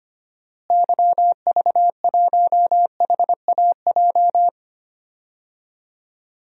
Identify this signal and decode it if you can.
Morse code, heard as a keyed tone, about 25 wpm, 700 Hz. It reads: YV1HAJ
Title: DXCC-100 Call Signs - 6 Characters - Repeated 2x - 25wpm